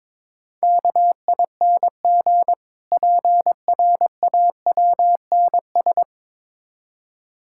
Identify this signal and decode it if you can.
Morse code, heard as a keyed tone, about 22 wpm, 690 Hz. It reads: KING PRAWNS
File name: Difficult Words with Hints - 22wpm